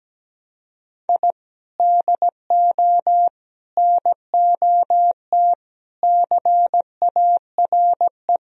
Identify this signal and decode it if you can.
Morse code, heard as a keyed tone, about 17 wpm, 700 Hz. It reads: I DO NOT CARE
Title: Sentences from Top 1000 Words - Review Part 3 - 17wpm